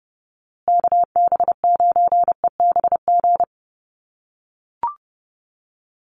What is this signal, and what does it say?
Morse code, heard as a keyed tone, about 30 wpm, 700 Hz. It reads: KB9EBG
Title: USA Call Signs Repeated 2x 30wpm